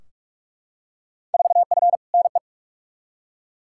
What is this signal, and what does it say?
Morse code, heard as a keyed tone, about 42 wpm, 700 Hz. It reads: 4F NE
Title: Field Day Exchanges Encore - 45wpm